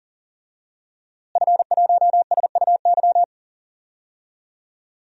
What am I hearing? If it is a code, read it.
Morse code, heard as a keyed tone, about 37 wpm, 700 Hz. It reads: F1SUY
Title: DXCC-100 Call Signs - 5 Characters - Repeated 2x - 40wpm